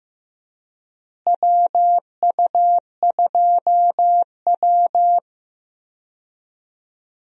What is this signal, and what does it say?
Morse code, heard as a keyed tone, about 15 wpm, 700 Hz. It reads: WU2W
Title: USA Call Signs - 4 Characters - Repeated 3x - 15wpm